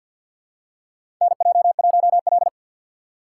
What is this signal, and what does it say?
Morse code, heard as a keyed tone, about 52 wpm, 700 Hz. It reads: NJ1L
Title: USA Call Signs - 4 Characters - 3x Character Spacing - 50wpm